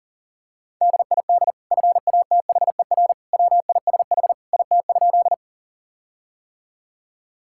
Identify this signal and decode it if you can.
Morse code, heard as a keyed tone, about 40 wpm, 700 Hz. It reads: DID FATHER WISH IT?
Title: Sentences from Top 1600 Words 40wpm